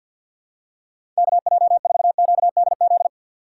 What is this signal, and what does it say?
Morse code, heard as a keyed tone, about 51 wpm, 700 Hz. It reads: KJ4QDZ
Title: USA Call Signs - 6 Characters - 4x Character Spacing - 50wpm